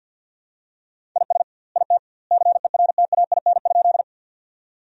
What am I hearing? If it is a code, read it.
Morse code, heard as a keyed tone, about 46 wpm, 700 Hz. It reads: IS IT CERTAIN?
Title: 2x Word-Spacing - Sentences from Top 500 Words - 50wpm